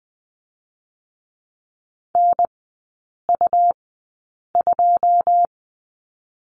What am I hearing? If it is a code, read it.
Morse code, heard as a keyed tone, about 20 wpm, 700 Hz. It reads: N U 2
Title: 100 Lesson 17e - Letter u - Three Characters 20wpm